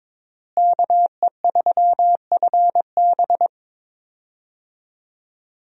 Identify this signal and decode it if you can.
Morse code, heard as a keyed tone, about 22 wpm, 700 Hz. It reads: KE3FB
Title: USA Call Signs - 5 Characters - Repeated 3x - 22wpm